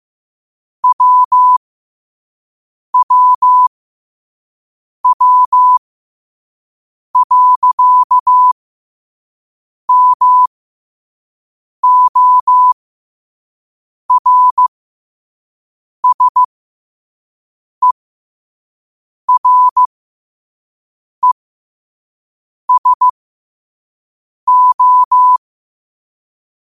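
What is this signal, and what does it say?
Morse code, heard as a keyed tone, about 15 wpm, spaced as for 5 wpm, 1000 Hz. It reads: WWW.MORSERESO
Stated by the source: Morse code practice at 5 wpm.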